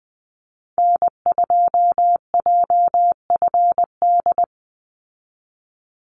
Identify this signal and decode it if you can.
Morse code, heard as a keyed tone, about 20 wpm, 700 Hz. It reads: N2JFD
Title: Call Signs Repeated 2x - 20wpm